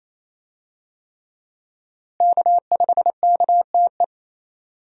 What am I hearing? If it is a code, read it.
Morse code, heard as a keyed tone, about 28 wpm, 700 Hz. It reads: K5KTE
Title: USA Call Signs - 5 Characters - Repeated 3x - 28wpm